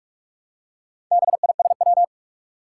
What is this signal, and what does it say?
Morse code, heard as a keyed tone, about 44 wpm, 700 Hz. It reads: BISW
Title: Four Challenging Letter-Number Combinations 45wpm